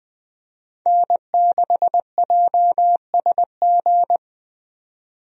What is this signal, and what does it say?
Morse code, heard as a keyed tone, about 21 wpm, 700 Hz. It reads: N6JSG
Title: USA Call Signs - 5 Characters - 20wpm